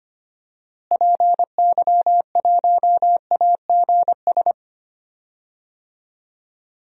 Morse code, heard as a keyed tone, about 25 wpm, 700 Hz. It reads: PY1AGS
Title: DXCC-100 Call Signs - 6 Characters - Repeated 2x - 25wpm